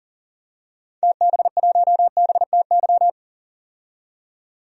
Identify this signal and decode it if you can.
Morse code, heard as a keyed tone, about 40 wpm, 700 Hz. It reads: TB1BTY
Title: DXCC-100 Call Signs - 6 Characters - 40wpm